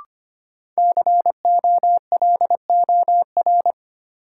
Morse code, heard as a keyed tone, about 25 wpm, 700 Hz. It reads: COLOR
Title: Sentences from Top 1000 Words - Review Part 2 - New Word Drill - 25wpm